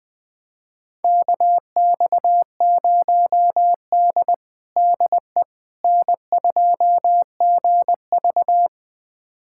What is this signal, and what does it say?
Morse code, heard as a keyed tone, about 20 wpm, 700 Hz. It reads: KX0D DE N2GV
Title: USA Callsign DE Callsign - 4 Characters - 20wpm